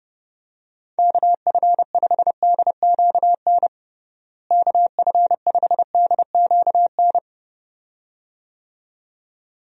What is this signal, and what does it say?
Morse code, heard as a keyed tone, about 32 wpm, 700 Hz. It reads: KF5DQN KF5DQN
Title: USA Call Signs - 6 Characters - Repeated 2x - 30wpm